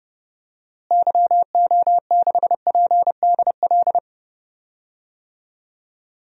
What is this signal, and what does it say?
Morse code, heard as a keyed tone, about 32 wpm, 700 Hz. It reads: YO6PDL
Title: DXCC-100 Call Signs - 6 Characters - Repeated 2x - 30wpm